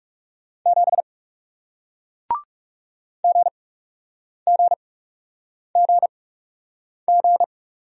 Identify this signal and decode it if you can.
Morse code, heard as a keyed tone, about 40 wpm, spaced as for 8 wpm, 700 Hz. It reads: 7 GGGG